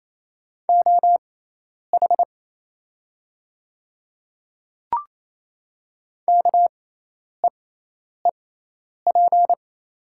Morse code, heard as a keyed tone, about 28 wpm, spaced as for 10 wpm, 700 Hz. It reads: OH KEEP
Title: 6x Character Spacing - Top 500 Words - 28wpm